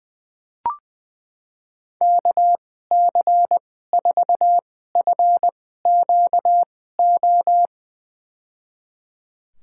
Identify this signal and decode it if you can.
Morse code, heard as a keyed tone, about 20 wpm, spaced as for 15 wpm, 700 Hz. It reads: KC4FQO
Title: USA Call Signs - 6 Characters - 2x Character Spacing - 20wpm